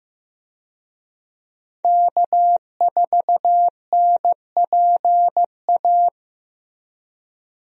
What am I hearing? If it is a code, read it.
Morse code, heard as a keyed tone, about 15 wpm, 700 Hz. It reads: K4NPA
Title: USA Call Signs - 5 Characters - Repeated 3x - 15wpm